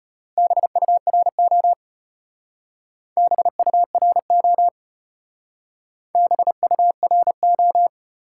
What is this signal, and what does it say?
Morse code, heard as a keyed tone, about 34 wpm, 700 Hz. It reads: BURO BURO BURO